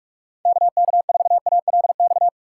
Morse code, heard as a keyed tone, about 45 wpm, 700 Hz. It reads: KK4ALX
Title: USA Call Signs - 4x Character Spacing - 45wpm